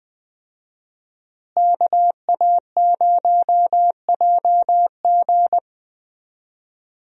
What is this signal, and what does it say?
Morse code, heard as a keyed tone, about 21 wpm, 700 Hz. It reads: KA0JG
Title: USA Call Signs - 5 Characters - Repeated 3x - 20wpm